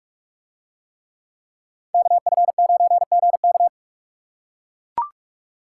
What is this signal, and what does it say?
Morse code, heard as a keyed tone, about 43 wpm, 700 Hz. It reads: KF9GK